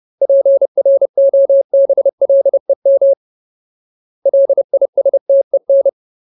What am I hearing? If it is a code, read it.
Morse code, heard as a keyed tone, about 30 wpm, 550 Hz. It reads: PROBLEM LISTEN
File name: Ultimate Word List / 30WPM Ultimate